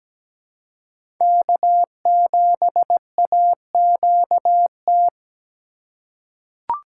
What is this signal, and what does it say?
Morse code, heard as a keyed tone, about 17 wpm, 700 Hz. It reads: K7AQT
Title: USA Call Signs - 5 Characters - 4x Character Spacing - 17wpm